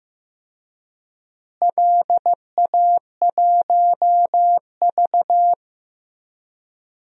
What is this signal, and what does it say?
Morse code, heard as a keyed tone, about 15 wpm, 700 Hz. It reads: LA1V
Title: DXCC-100 Call Signs - 4 Characters - Repeated 2x - 15wpm